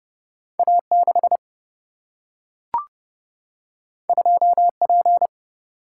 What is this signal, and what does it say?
Morse code, heard as a keyed tone, about 30 wpm, 700 Hz. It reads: A 6 2 P
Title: Single-Letter Single-Number 30wpm